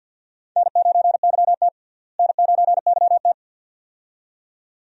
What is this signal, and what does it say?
Morse code, heard as a keyed tone, about 53 wpm, 690 Hz. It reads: N9YT N9YT
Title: USA Call Signs - 4 Characters - Repeated 2x - 50wpm